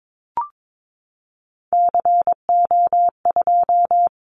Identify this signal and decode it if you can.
Morse code, heard as a keyed tone, about 22 wpm, 700 Hz. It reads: CO2
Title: Callsign Prefixes for Top 100 DXCC - 22wpm